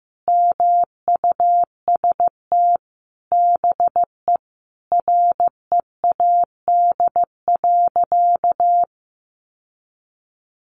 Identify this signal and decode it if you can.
Morse code, heard as a keyed tone, about 15 wpm, 700 Hz. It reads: MUST BE READ.
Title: Sentences from Top 1100 Words 15wpm